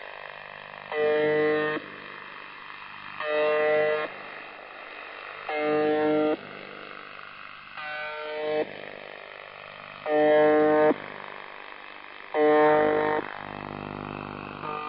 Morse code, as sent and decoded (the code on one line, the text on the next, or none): none